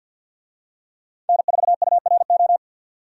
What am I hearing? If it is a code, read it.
Morse code, heard as a keyed tone, about 51 wpm, 700 Hz. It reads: N4URO
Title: USA Call Signs - 5 Characters - 50wpm